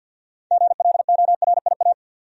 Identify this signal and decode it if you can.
Morse code, heard as a keyed tone, about 48 wpm, 700 Hz. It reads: GLORIA
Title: USA Common Names - 50wpm